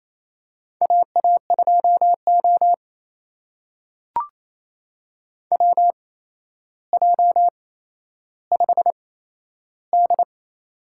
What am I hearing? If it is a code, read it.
Morse code, heard as a keyed tone, about 28 wpm, spaced as for 12 wpm, 700 Hz. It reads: AA2O WJ5D
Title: USA Call Signs - 8x Character Spacing - 28wpm